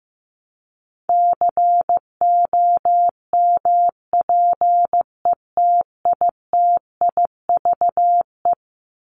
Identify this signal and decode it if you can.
Morse code, heard as a keyed tone, about 15 wpm, 700 Hz. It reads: COMPETITIVE
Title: Top 1001-1250 Words 15wpm